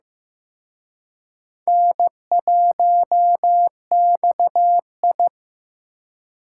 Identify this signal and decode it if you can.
Morse code, heard as a keyed tone, about 18 wpm, 700 Hz. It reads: N1XI
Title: Speed Racing - USA Call Signs 15wpm